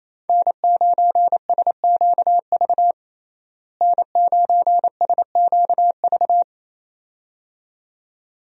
Morse code, heard as a keyed tone, about 28 wpm, 700 Hz. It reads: N9SQV N9SQV
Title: USA Call Signs - 5 Characters - Repeated 2x - 28wpm